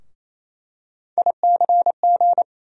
Morse code, heard as a keyed tone, about 28 wpm, 700 Hz. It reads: ICG
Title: USA Call Sign Postfixes - 3 Letters 28wpm